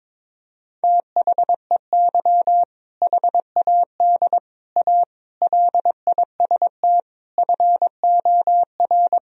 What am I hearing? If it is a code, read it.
Morse code, heard as a keyed tone, about 22 wpm, 700 Hz. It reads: THEY HAD A LIST FOR
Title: Sentences from Top 1000 Words - Review Part 2 - 22wpm